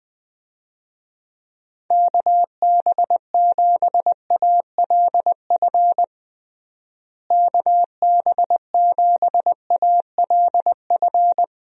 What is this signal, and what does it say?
Morse code, heard as a keyed tone, about 20 wpm, 700 Hz. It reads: KB7ALF KB7ALF
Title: USA Call Signs - 6 Characters - Repeated 3x - 20wpm